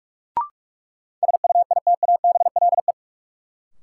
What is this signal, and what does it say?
Morse code, heard as a keyed tone, about 46 wpm, 700 Hz. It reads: SUITABLE
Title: Sentences from Top 1400 Words - New Word Drill Alt - 45wpm